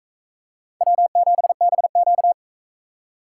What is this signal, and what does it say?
Morse code, heard as a keyed tone, about 37 wpm, 700 Hz. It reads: W7BQ